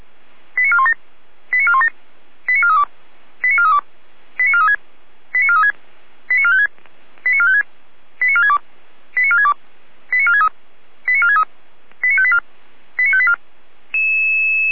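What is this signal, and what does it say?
Five-tone selective call (ZVEI), heard as quick runs of five tones, tones of about 70 ms. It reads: 89417 89417 89432 89432 89436 89436 89456 89456 89462 89462 89463 89463 89584 89584
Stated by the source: Received by SDR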